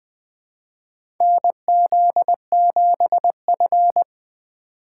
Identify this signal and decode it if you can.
Morse code, heard as a keyed tone, about 21 wpm, 700 Hz. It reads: NZ7F
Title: USA Call Signs - 3x Character Spacing - 20wpm